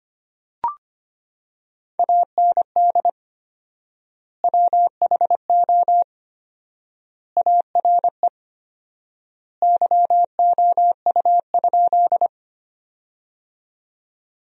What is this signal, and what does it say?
Morse code, heard as a keyed tone, about 25 wpm, 700 Hz. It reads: AND WHO ARE YOU?
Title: 4x Word-Spacing - Sentences from Top 100 Words - 25wpm